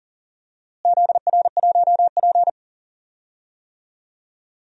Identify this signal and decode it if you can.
Morse code, heard as a keyed tone, about 40 wpm, 700 Hz. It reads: ZR1P
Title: DXCC-100 Call Signs - 4 Characters - Repeated 3x - 40wpm